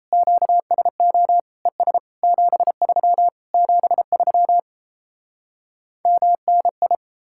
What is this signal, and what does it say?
Morse code, heard as a keyed tone, about 38 wpm, 700 Hz. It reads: QSO ES 73 73 MNI